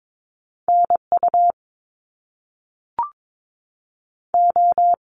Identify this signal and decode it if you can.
Morse code, heard as a keyed tone, about 22 wpm, 700 Hz. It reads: NU O
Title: Canadian Provinces and Territories 22wpm